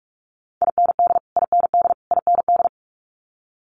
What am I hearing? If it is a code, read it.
Morse code, heard as a keyed tone, about 46 wpm, 700 Hz. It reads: IND IND IND
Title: Multi-letter Phonemes Repeated 3x 45wpm